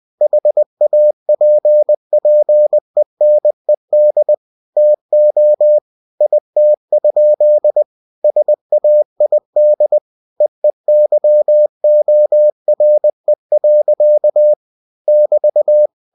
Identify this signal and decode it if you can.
Morse code, heard as a keyed tone, about 20 wpm, 600 Hz. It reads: HAPPENED TO IT? SAID EEYORE. <BT>
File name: cw-books / winnie-the-pooh / winnie-20wpm-page-0027